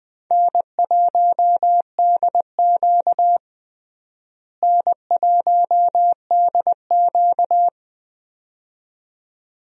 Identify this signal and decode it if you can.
Morse code, heard as a keyed tone, about 21 wpm, 700 Hz. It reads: N1DQ N1DQ
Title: USA Call Signs - 4 Characters - Repeated 2x - 20wpm